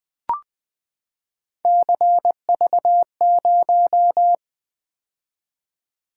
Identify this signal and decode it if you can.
Morse code, heard as a keyed tone, about 20 wpm, 700 Hz. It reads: CV0
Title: Three Letter-Number Combinations - 20wpm